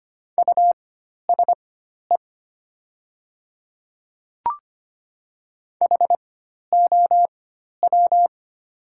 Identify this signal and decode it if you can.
Morse code, heard as a keyed tone, about 25 wpm, spaced as for 12 wpm, 700 Hz. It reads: USE HOW
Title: 4x Character Spacing - Top 500 Words - 25wpm